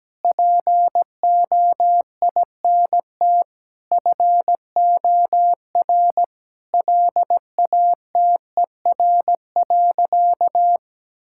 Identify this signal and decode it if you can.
Morse code, heard as a keyed tone, about 17 wpm, 700 Hz. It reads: POINT FOR LATER.
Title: Sentences from Top 1000 Words - Review Part 6 - 17wpm